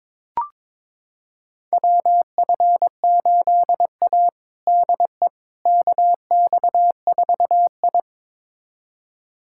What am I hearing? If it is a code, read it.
Morse code, heard as a keyed tone, about 22 wpm, 700 Hz. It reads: WF8A DE KX4I
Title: USA Callsign DE Callsign - 4 Characters - 22wpm